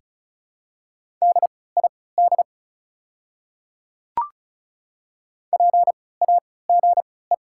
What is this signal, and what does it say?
Morse code, heard as a keyed tone, about 36 wpm, spaced as for 20 wpm, 690 Hz. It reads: DID PAGE
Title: 3x Character Spacing - Top 500 Words - 35wpm